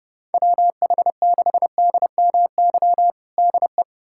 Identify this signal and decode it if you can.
Morse code, heard as a keyed tone, about 30 wpm, 700 Hz. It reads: WH6DMY DE
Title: USA Callsign DE Callsign - 6 Characters - 30wpm